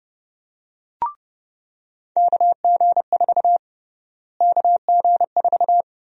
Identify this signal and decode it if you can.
Morse code, heard as a keyed tone, about 29 wpm, 700 Hz. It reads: KG4 KG4
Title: USA Call Sign Prefixes - Repeated 2x - 30wpm